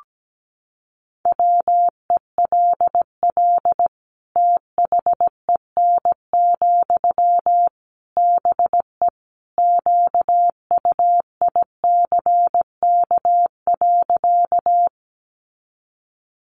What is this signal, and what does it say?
Morse code, heard as a keyed tone, about 17 wpm, 700 Hz. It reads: WELL THEN, BE QUICK.
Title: Sentences from Top 500 Words 17wpm